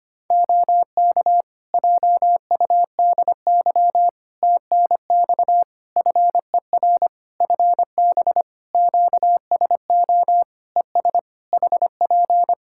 Morse code, heard as a keyed tone, about 25 wpm, 700 Hz. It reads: OK JUDY TNX FER FB QSO ES HP
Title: Practice QSO - Exchange 4 - Ending - 25wpm